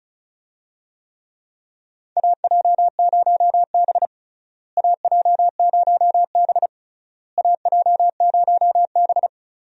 Morse code, heard as a keyed tone, about 34 wpm, 700 Hz. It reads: AJ0B AJ0B AJ0B
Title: USA Call Signs - 4 Characters - Repeated 3x - 35wpm